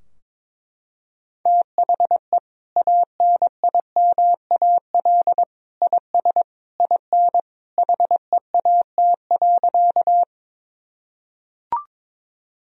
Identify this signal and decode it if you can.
Morse code, heard as a keyed tone, about 22 wpm, 700 Hz. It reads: THE ANIMAL IS IN HEAT.
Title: Sentences from Top 1000 Words - Review Part 3 - 22wpm